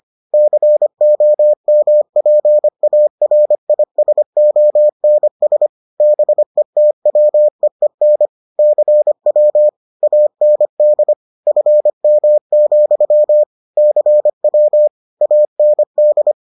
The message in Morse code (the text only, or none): COMPARISONS BETWEEN CW AND FM, CW AND